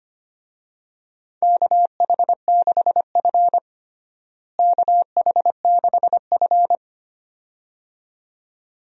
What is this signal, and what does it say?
Morse code, heard as a keyed tone, about 25 wpm, 700 Hz. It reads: KH6F KH6F
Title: DXCC-100 Call Signs - 4 Characters - Repeated 2x - 25wpm